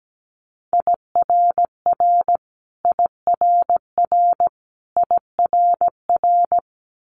Morse code, heard as a keyed tone, about 17 wpm, 700 Hz. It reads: IRR IRR IRR
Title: Multi-letter Phonemes Repeated 3x 17wpm